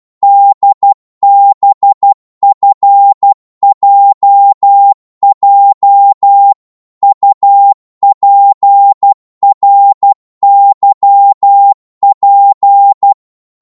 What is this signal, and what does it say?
Morse code, heard as a keyed tone, about 12 wpm, 800 Hz. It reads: DBFJJ UPRYP
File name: Morse / ltr-12wpm